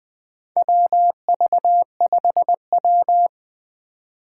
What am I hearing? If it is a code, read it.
Morse code, heard as a keyed tone, about 20 wpm, 700 Hz. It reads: WV5W
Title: USA Call Signs - 4 Characters - 3x Character Spacing - 20wpm